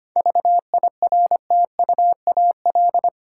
Morse code, heard as a keyed tone, about 25 wpm, 690 Hz. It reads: VIRTUAL
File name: practice / 25wpm / Difficult Words with Hints - 25wpm